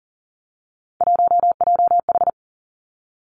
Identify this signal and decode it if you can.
Morse code, heard as a keyed tone, about 38 wpm, 700 Hz. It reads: 1JH
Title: Three Challenging Letter-Number Combinations 40wpm